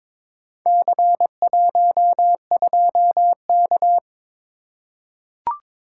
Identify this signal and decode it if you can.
Morse code, heard as a keyed tone, about 22 wpm, 700 Hz. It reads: C12K
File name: practice / 22wpm / Four Letter-Number Combinations - 22wpm